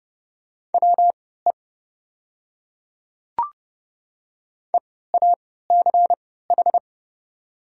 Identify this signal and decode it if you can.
Morse code, heard as a keyed tone, about 30 wpm, spaced as for 17 wpm, 690 Hz. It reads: WE EACH
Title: 3x Character Spacing - Top 200 Words - 30wpm